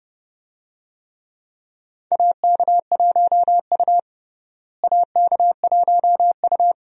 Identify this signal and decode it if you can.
Morse code, heard as a keyed tone, about 32 wpm, 700 Hz. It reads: AK1U AK1U
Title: USA Call Signs - 4 Characters - Repeated 2x - 30wpm